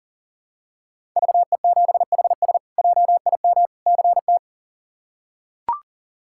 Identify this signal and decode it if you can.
Morse code, heard as a keyed tone, about 40 wpm, 700 Hz. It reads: VE7HS JIM CT
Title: K1USN SST QSO Practice - 40wpm